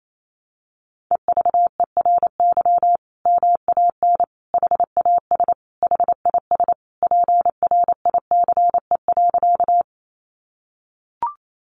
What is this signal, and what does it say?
Morse code, heard as a keyed tone, about 28 wpm, 700 Hz. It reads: EVERY MAN HAS HIS PRICE.
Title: Sentences from Top 600 Words 28wpm